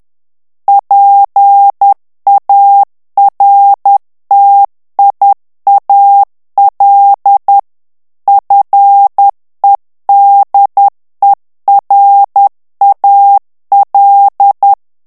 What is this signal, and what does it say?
Morse code, heard as a keyed tone, about 10 wpm, 800 Hz. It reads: PARTIAL FEDERAL
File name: ANGLAIS texte clair 3 vitesse 10 mots minute